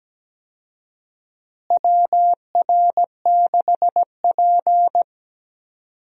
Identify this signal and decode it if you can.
Morse code, heard as a keyed tone, about 17 wpm, 700 Hz. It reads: WR6P
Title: USA Call Signs - 4 Characters - Repeated 2x - 17wpm